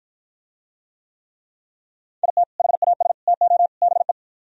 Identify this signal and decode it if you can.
Morse code, heard as a keyed tone, about 52 wpm, 700 Hz. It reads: IT HAS TO BE